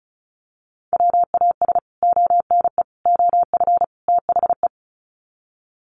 Morse code, heard as a keyed tone, about 34 wpm, 700 Hz. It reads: WAS ONE OF THE
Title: Sets of 4 Words 35wpm